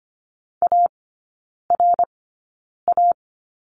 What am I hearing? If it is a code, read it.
Morse code, heard as a keyed tone, about 25 wpm, spaced as for 9 wpm, 690 Hz. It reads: ARA
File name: Lesson 14e - Number 2 - Three Characters 25wpm